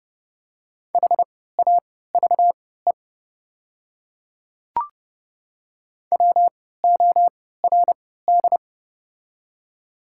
Morse code, heard as a keyed tone, about 30 wpm, spaced as for 17 wpm, 700 Hz. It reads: HAVE WORD
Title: 3x Character Spacing - Top 100 Words - 30wpm